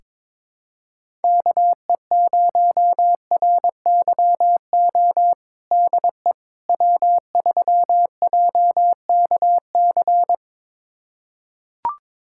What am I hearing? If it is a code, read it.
Morse code, heard as a keyed tone, about 22 wpm, 700 Hz. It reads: KE0RYO DE W3JKC
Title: USA Callsign DE Callsign 22wpm